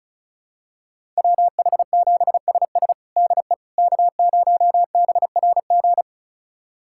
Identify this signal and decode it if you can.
Morse code, heard as a keyed tone, about 36 wpm, 700 Hz. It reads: WH7SS DE K0BRG
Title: USA Callsign DE Callsign - 5 Characters - 35wpm